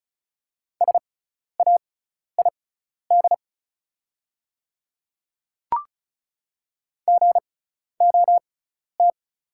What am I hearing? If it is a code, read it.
Morse code, heard as a keyed tone, about 36 wpm, spaced as for 12 wpm, 690 Hz. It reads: SAID GOT
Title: 6x Character Spacing - Top 300 Words - 35wpm